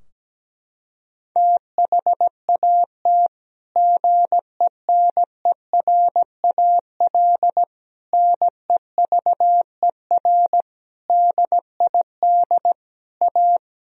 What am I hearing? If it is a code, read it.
Morse code, heard as a keyed tone, about 17 wpm, 700 Hz. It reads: THAT GENERAL NEVER DID A
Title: Sentences from Top 1000 Words - Review Part 4 - 17wpm